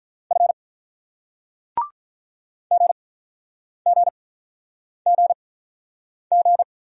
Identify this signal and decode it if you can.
Morse code, heard as a keyed tone, about 48 wpm, spaced as for 9 wpm, 700 Hz. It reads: F GGGG